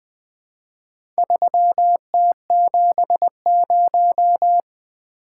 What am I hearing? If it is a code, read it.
Morse code, heard as a keyed tone, about 20 wpm, 700 Hz. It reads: 3T70
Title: Four Letter-Number Combinations - 20wpm